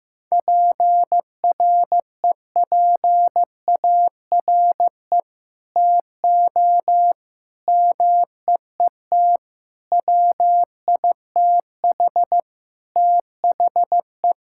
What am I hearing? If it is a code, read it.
Morse code, heard as a keyed tone, about 15 wpm, 700 Hz. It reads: PREPARE TO MEET WITH THE
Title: Sentences from Top 1000 Words - Review Part 6 - 15wpm